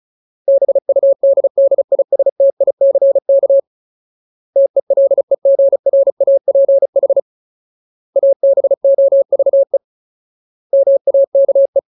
Morse code, heard as a keyed tone, about 34 wpm, 550 Hz. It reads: BUDDISTICK TELEGRAPH ABOVE MAKE
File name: Ultimate Word List / 35WPM Ultimate